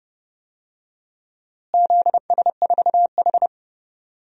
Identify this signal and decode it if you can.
Morse code, heard as a keyed tone, about 31 wpm, 700 Hz. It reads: ZS4H
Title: DXCC-100 Call Signs - 4 Characters - Repeated 3x - 30wpm